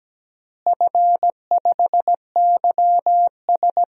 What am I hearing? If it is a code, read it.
Morse code, heard as a keyed tone, about 17 wpm, 700 Hz. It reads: F5YS
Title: DXCC-100 Call Signs - 4 Characters - Repeated 2x - 17wpm